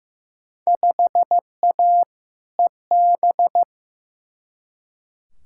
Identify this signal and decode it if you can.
Morse code, heard as a keyed tone, about 15 wpm, 700 Hz. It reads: 5A EB
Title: Field Day Exchanges Encore - 15wpm